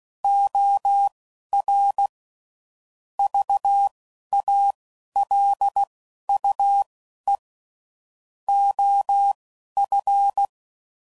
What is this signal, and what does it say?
Morse code, heard as a keyed tone, about 16 wpm, spaced as for 11 wpm, 800 Hz. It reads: OR VALUE OF
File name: CW Code Groups @10WPM / Technical Subjects / XC